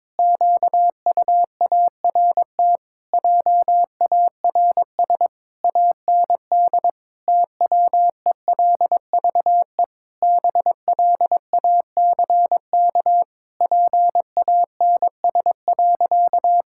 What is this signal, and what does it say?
Morse code, heard as a keyed tone, about 22 wpm, 700 Hz. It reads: QUART JARS AND TWELVE BLACK PANS.
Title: Pangram Challenge v2 - 22wpm